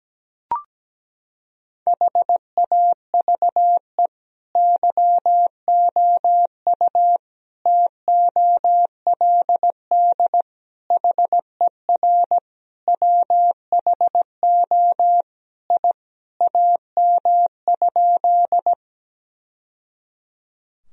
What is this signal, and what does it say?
Morse code, heard as a keyed tone, about 17 wpm, 700 Hz. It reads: HAVE YOU TOLD HER WHO I AM?
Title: Sentences from Top 1000 Words - Review Part 2 - 17wpm